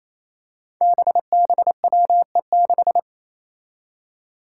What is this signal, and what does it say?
Morse code, heard as a keyed tone, about 27 wpm, 700 Hz. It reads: BBWE6
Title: Five Challenging Letter-Number Combinations 28wpm